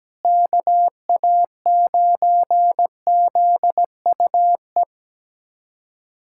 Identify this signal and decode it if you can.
Morse code, heard as a keyed tone, about 17 wpm, 700 Hz. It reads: KA9ZUE
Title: USA Callsign DE Callsign 17wpm